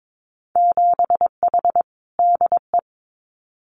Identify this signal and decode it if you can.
Morse code, heard as a keyed tone, about 22 wpm, 700 Hz. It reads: 7H DE
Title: Winter Field Day Exchanges Encore 22wpm